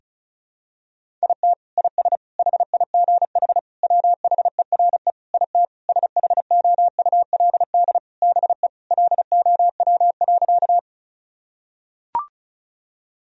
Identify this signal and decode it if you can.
Morse code, heard as a keyed tone, about 34 wpm, 700 Hz. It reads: IT IS HIGH WHERE IT SHOULD BE LOW.
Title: Sentences from Top 1000 Words - Review Part 1 - 35wpm